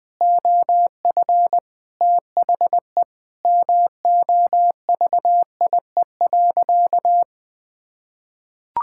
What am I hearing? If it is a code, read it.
Morse code, heard as a keyed tone, about 20 wpm, 700 Hz. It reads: OF THE MOVIE.
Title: Sentences with 6-Word Phrases - Spoken Beginning - 20wpm